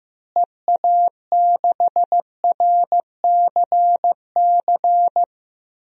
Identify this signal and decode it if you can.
Morse code, heard as a keyed tone, about 15 wpm, 700 Hz. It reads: EA6RCC
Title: DXCC-100 Call Signs - 6 Characters - Repeated 3x - 15wpm